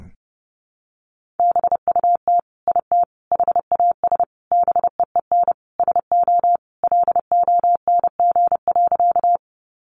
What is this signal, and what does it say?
Morse code, heard as a keyed tone, about 30 wpm, 700 Hz. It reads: BUT IT HAS BEEN SO LONG.
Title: Sentences from Top 100 Words 30wpm